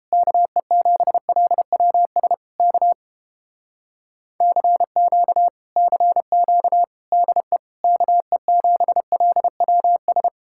Morse code, heard as a keyed tone, about 29 wpm, 700 Hz. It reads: KE7LWS K CQ CQ DE KE7LWS